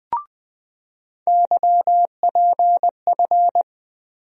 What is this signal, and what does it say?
Morse code, heard as a keyed tone, about 21 wpm, 700 Hz. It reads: YPF
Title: USA Call Sign Postfixes - 3 Letters 20wpm